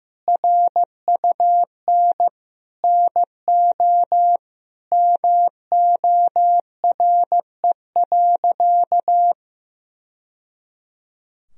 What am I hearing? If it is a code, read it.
Morse code, heard as a keyed tone, about 16 wpm, 700 Hz. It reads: RUN NO MORE.
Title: Sentences from Top 1000 Words - Review Part 1 - 15wpm